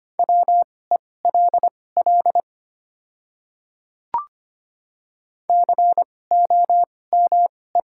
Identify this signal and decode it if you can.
Morse code, heard as a keyed tone, about 25 wpm, spaced as for 18 wpm, 700 Hz. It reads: WELL COME
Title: 2x Character Spacing - Top 300 Words - 25wpm